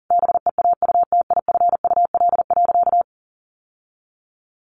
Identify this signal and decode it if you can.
Morse code, heard as a keyed tone, about 40 wpm, 700 Hz. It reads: BEAUTIFUL.
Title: Sentences from Top 1100 Words 40wpm